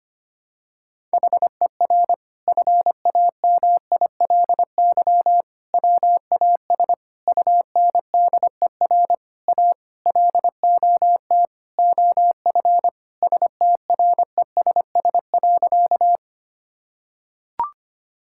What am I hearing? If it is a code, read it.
Morse code, heard as a keyed tone, about 25 wpm, 700 Hz. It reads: HER FAMILY WAS UNDER A LOT OF STRESS.
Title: Sentences from Top 1000 Words - Review Part 6 - 25wpm